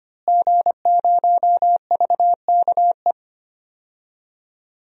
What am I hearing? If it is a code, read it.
Morse code, heard as a keyed tone, about 25 wpm, 700 Hz. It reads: G0VKE
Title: DXCC-100 Call Signs - 5 Characters - Repeated 3x - 25wpm